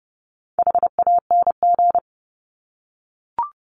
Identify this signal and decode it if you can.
Morse code, heard as a keyed tone, about 30 wpm, 700 Hz. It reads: HANG
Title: Top 1501-1750 Words 30wpm